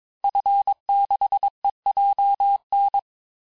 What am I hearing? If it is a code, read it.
Morse code, heard as a keyed tone, about 22 wpm, 800 Hz. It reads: F6EJN
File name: cours-du-mercredi / audios / cours-854-liste-fin-QSO-vitesse-21-mpm